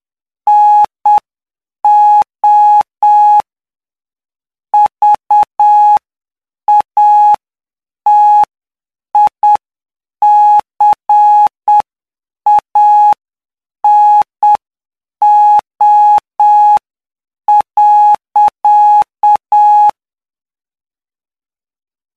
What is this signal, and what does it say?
Morse code, heard as a keyed tone, about 7 wpm, spaced as for 6 wpm, 840 Hz. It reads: NO VATICANO.
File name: cw / py2lcd / 15-PY2LCD-Li��o 24